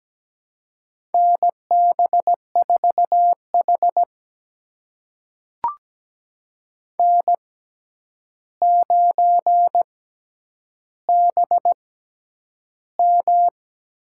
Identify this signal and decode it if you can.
Morse code, heard as a keyed tone, about 17 wpm, spaced as for 9 wpm, 700 Hz. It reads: NB4H N9BM
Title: USA Call Signs - 4 Characters - 6x Character Spacing - 17wpm